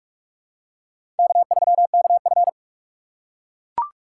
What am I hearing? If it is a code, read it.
Morse code, heard as a keyed tone, about 44 wpm, 700 Hz. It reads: K3KF
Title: USA Call Signs - 3x Character Spacing - 45wpm